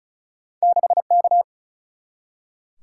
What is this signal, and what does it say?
Morse code, heard as a keyed tone, about 36 wpm, 700 Hz. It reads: 6K
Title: Single-Letter Single-Number 35wpm